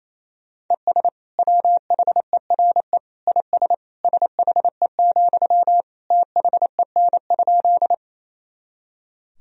Morse code, heard as a keyed tone, about 27 wpm, 700 Hz. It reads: ES WHERE IS SHE, THEN?
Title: Sentences from Top 200 Words and Abbreviations 28wpm